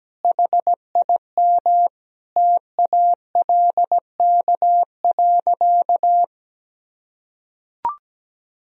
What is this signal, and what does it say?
Morse code, heard as a keyed tone, about 17 wpm, 700 Hz. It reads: HIM TALK.
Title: Sentences from Top 1000 Words - Review Part 4 - 17wpm